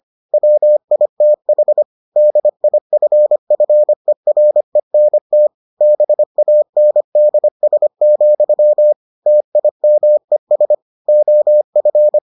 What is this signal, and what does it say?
Morse code, heard as a keyed tone, about 25 wpm, 600 Hz. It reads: WITH DIFFERENT BANDS, TIMES OF